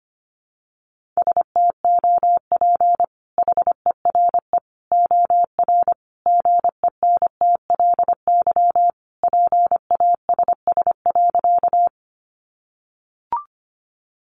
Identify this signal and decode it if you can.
Morse code, heard as a keyed tone, about 25 wpm, 700 Hz. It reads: STOP HERE OR GENTLY PASS.
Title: Sentences from Top 1500 Words 25wpm